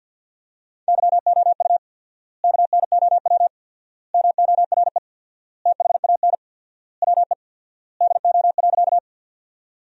Morse code, heard as a keyed tone, about 50 wpm, 700 Hz. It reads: YOU KNOW MORE THAN WE DO.
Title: 4x Word-Spacing - Sentences from Top 100 Words - 50wpm